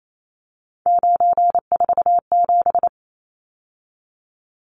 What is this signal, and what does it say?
Morse code, heard as a keyed tone, about 28 wpm, 700 Hz. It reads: 947
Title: Numbers - 3-Digit 28wpm